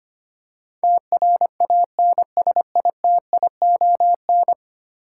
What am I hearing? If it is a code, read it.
Morse code, heard as a keyed tone, about 25 wpm, 700 Hz. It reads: TRANSITION
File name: 10 Letter Words - First 3 Letters Given - 25wpm